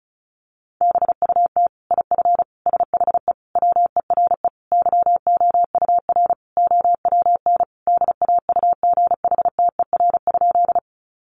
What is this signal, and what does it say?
Morse code, heard as a keyed tone, about 34 wpm, 700 Hz. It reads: BUT IF SHE WERE YOUR OWN DAUGHTER?
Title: Sentences from Top 1500 Words 35wpm